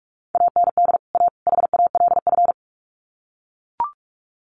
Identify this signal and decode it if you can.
Morse code, heard as a keyed tone, about 44 wpm, 700 Hz. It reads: AND A HALF
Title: Sets of 3 Words 45wpm